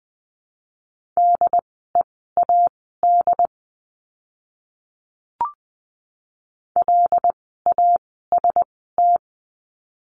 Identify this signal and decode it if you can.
Morse code, heard as a keyed tone, about 20 wpm, spaced as for 14 wpm, 700 Hz. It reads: DEAD LAST
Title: 075 Lesson 13f - Letter L - Words 20wpm